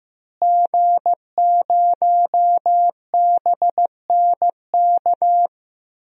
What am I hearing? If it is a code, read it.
Morse code, heard as a keyed tone, about 16 wpm, 700 Hz. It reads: G0BNK
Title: DXCC-100 Call Signs - 5 Characters - Repeated 2x - 15wpm